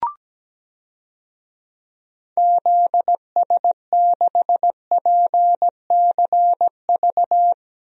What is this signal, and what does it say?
Morse code, heard as a keyed tone, about 17 wpm, 700 Hz. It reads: ZS6PCV
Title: DXCC-100 Call Signs - 6 Characters - Repeated 2x - 17wpm